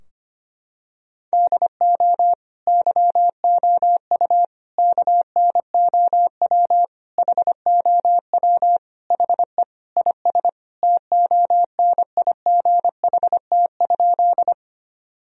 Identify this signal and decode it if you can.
Morse code, heard as a keyed tone, about 25 wpm, 700 Hz. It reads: DO YOU KNOW HOW HE IS TONIGHT?
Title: Sentences from Top 1000 Words - Review Part 3 - 25wpm